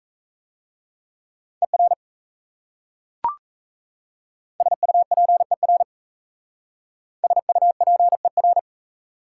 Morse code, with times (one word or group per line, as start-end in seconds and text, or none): 1.6-1.9 ER
4.6-5.8 SUPER
7.2-8.6 SUPER